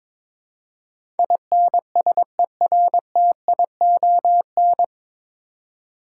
Morse code, heard as a keyed tone, about 23 wpm, 700 Hz. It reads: INSERTION
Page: Morse Code Ninja practice 22wpm